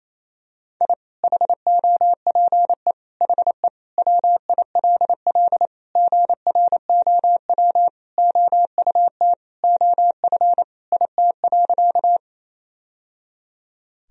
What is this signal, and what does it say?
Morse code, heard as a keyed tone, about 28 wpm, 700 Hz. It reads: I HOPE HE WILL GROW OUT OF IT.
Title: Sentences from Top 1000 Words - Review Part 1 - 28wpm